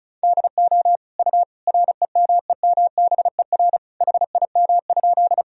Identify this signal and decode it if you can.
Morse code, heard as a keyed tone, about 36 wpm, 700 Hz. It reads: DO U REMEMBER HIM?
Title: Sentences from Top 400 Words and Abbreviations 35wpm